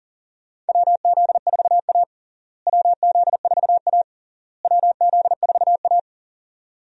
Morse code, heard as a keyed tone, about 37 wpm, 700 Hz. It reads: WZ4A WZ4A WZ4A
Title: USA Call Signs - 4 Characters - Repeated 3x - 40wpm